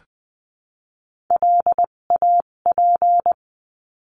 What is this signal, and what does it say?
Morse code, heard as a keyed tone, about 20 wpm, spaced as for 17 wpm, 700 Hz. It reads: LAP
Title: 257 Lesson 42b - Words 14-20wpm